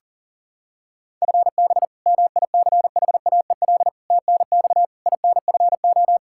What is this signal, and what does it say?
Morse code, heard as a keyed tone, about 37 wpm, 700 Hz. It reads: FB MICHAEL TNX INFO
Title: Practice QSO - Exchange 3 - More Info - 40wpm